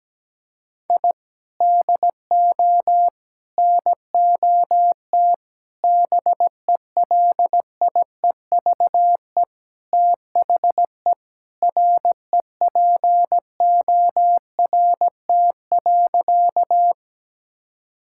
Morse code, heard as a keyed tone, about 17 wpm, 700 Hz. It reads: I DO NOT BELIEVE THE REPORT.
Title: Sentences from Top 1000 Words - Review Part 6 - 17wpm